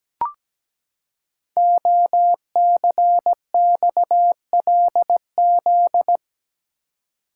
Morse code, heard as a keyed tone, about 17 wpm, 700 Hz. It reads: OCXLZ
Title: Code Groups - 5 Letters - 17wpm